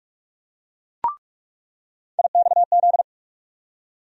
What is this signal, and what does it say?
Morse code, heard as a keyed tone, about 46 wpm, 700 Hz. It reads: IXZ
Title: USA Call Sign Postfixes - 3 Letters 45wpm